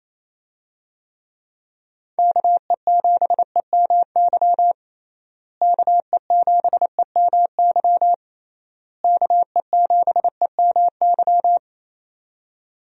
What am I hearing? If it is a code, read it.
Morse code, heard as a keyed tone, about 27 wpm, 700 Hz. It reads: KE7EMY KE7EMY KE7EMY
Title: USA Call Signs - 6 Characters - Repeated 3x - 28wpm